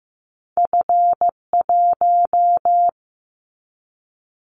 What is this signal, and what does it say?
Morse code, heard as a keyed tone, about 15 wpm, 700 Hz. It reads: F1
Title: Callsign Prefixes for Top 25 DXCC - Repeated 3x - 15wpm